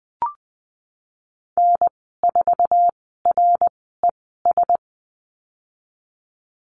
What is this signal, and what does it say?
Morse code, heard as a keyed tone, about 20 wpm, spaced as for 15 wpm, 700 Hz. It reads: N4RES
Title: 062 Lesson 11g - Letter h - Call Signs 20wpm